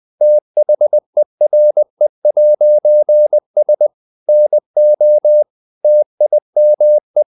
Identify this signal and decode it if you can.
Morse code, heard as a keyed tone, about 20 wpm, 600 Hz. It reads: THERE'S NO TIME
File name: CW / Practice / EnglishSayings20